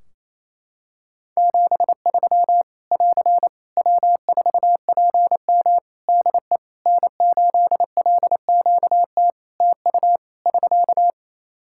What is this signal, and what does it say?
Morse code, heard as a keyed tone, about 27 wpm, 700 Hz. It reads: 73 <AR> W4PM DE N8LQT TU <SK>
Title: Practice QSO - Exchange 4 - Ending - 28wpm